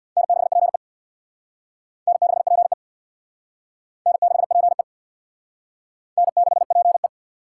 Morse code, heard as a keyed tone, about 64 wpm, 700 Hz. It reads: N6PE N6PE N6PE N6PE